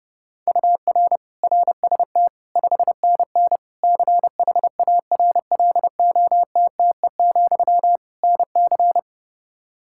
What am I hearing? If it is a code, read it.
Morse code, heard as a keyed tone, about 32 wpm, 700 Hz. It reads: UR RST 5NN CHARLOTTE, NC
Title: K1USN SST QSO Practice 30wpm